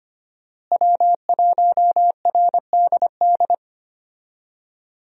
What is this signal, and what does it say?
Morse code, heard as a keyed tone, about 25 wpm, 700 Hz. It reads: W1RDD
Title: USA Call Signs - 5 Characters - Repeated 3x - 25wpm